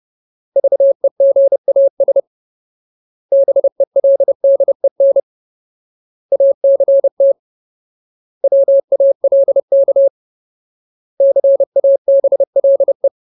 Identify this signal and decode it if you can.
Morse code, heard as a keyed tone, about 30 wpm, 550 Hz. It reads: VEGAS BELDEN ACT WALK CABLE
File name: Ultimate Word List / 30WPM Ultimate